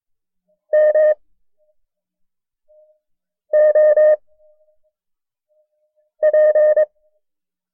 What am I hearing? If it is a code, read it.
Morse code, heard as a keyed tone, about 22 wpm, spaced as for 4 wpm, 610 Hz. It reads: MOP
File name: FreeFiles / CW-practice / KB6MT-tape1-side2